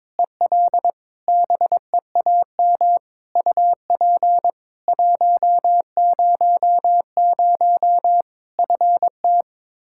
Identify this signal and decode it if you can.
Morse code, heard as a keyed tone, about 22 wpm, 700 Hz. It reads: EL BEAM UP 100 FT
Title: Practice QSO - Exchange 2 - Info - 22wpm